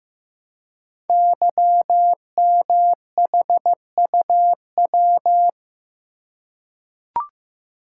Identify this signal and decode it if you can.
Morse code, heard as a keyed tone, about 15 wpm, 700 Hz. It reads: YMHUW
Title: Code Groups - 5 Letter:Numbers - 15wpm